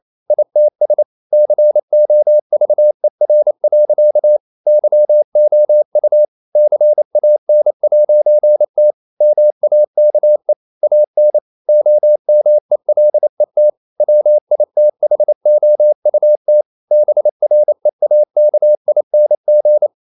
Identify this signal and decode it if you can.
Morse code, heard as a keyed tone, about 28 wpm, 600 Hz. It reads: ITS COVER. YOU CAN'T MAKE AN OMELET WITHOUT BREAKING
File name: CW / Practice / EnglishSayings28